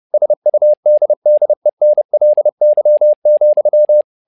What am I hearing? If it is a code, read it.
Morse code, heard as a keyed tone, about 30 wpm, 600 Hz. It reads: SUDDENLY,
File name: learning center / tower trouble / tower trouble 30wpm